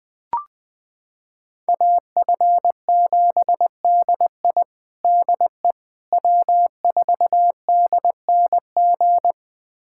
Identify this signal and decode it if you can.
Morse code, heard as a keyed tone, about 20 wpm, 700 Hz. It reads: AF7DI DE W4DNG
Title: USA Callsign DE Callsign - 5 Characters - 20wpm